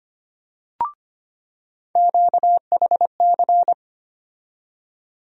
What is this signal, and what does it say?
Morse code, heard as a keyed tone, about 25 wpm, 700 Hz. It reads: QHC
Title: USA Call Sign Postfixes - 3 Letters 25wpm